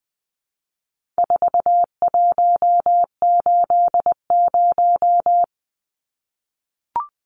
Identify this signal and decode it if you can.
Morse code, heard as a keyed tone, about 20 wpm, 700 Hz. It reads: 4180
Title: Numbers - 4-Digit 20wpm